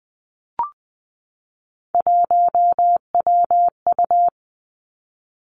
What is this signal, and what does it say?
Morse code, heard as a keyed tone, about 20 wpm, 700 Hz. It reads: 1WU
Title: Three Challenging Letter-Number Combinations 20wpm